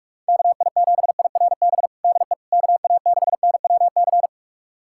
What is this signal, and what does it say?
Morse code, heard as a keyed tone, about 46 wpm, 700 Hz. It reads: KI7IRB DE KA6NWC
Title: USA Callsign DE Callsign - 6 Characters - 45wpm